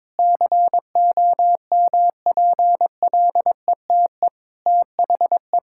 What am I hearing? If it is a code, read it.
Morse code, heard as a keyed tone, about 22 wpm, 700 Hz. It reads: COMPLETE THE
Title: Sentences from Top 1000 Words - Review Part 5 - 22wpm